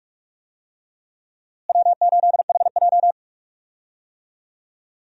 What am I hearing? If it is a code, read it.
Morse code, heard as a keyed tone, about 43 wpm, 700 Hz. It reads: W8HJ
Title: USA Call Signs - 4 Characters - Repeated 2x - 45wpm